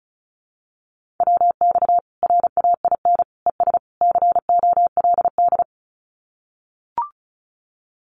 Morse code, heard as a keyed tone, about 34 wpm, 700 Hz. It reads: WX RAIN ES COLD
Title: Common QSO Phrases 35wpm